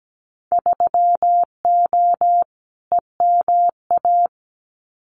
Morse code, heard as a keyed tone, about 17 wpm, 700 Hz. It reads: 3O EMA
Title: Winter Field Day Exchanges Encore 17wpm